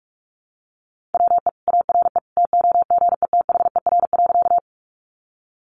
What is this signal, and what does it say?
Morse code, heard as a keyed tone, about 44 wpm, 700 Hz. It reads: WE ARE TOGETHER.
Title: Sentences from Top 300 Words 45wpm